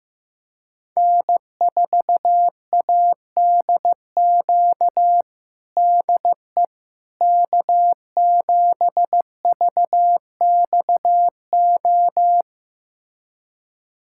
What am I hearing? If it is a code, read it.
Morse code, heard as a keyed tone, about 15 wpm, 700 Hz. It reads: N4ADQ DE K7VXO
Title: USA Callsign DE Callsign - 5 Characters - 15wpm